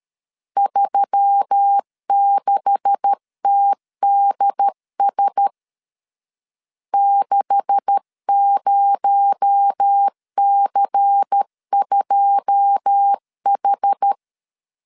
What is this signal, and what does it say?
Morse code, heard as a keyed tone, about 13 wpm, 800 Hz. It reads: 36TDS 60C2H
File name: LEZIONI / 9 286 12wpm 2305